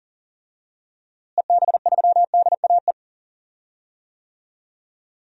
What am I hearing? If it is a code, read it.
Morse code, heard as a keyed tone, about 40 wpm, 700 Hz. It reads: EB3DAE
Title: DXCC-100 Call Signs - 6 Characters - Repeated 3x - 40wpm